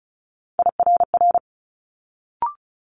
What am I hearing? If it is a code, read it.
Morse code, heard as a keyed tone, about 36 wpm, 700 Hz. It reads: IRR
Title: Multi-letter Phonemes Quick Fire 35wpm